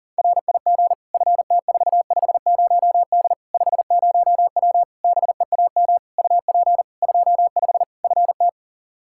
Morse code, heard as a keyed tone, about 40 wpm, 700 Hz. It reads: RIG FT450D 50W BEAM UP 25 FT
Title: Practice QSO - Exchange 2 - Info - 40wpm